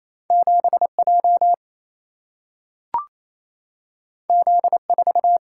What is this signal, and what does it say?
Morse code, heard as a keyed tone, about 28 wpm, 700 Hz. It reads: 7J Z4
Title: Single-Letter Single-Number 28wpm